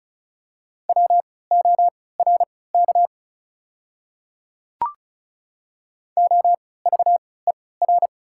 Morse code, heard as a keyed tone, about 36 wpm, spaced as for 20 wpm, 690 Hz. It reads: WORK OVER
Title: 3x Character Spacing - Top 200 Words - 35wpm